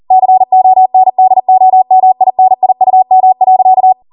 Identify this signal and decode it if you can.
Morse code, heard as a keyed tone, about 40 wpm, 750 Hz. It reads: CONDOMINIUM.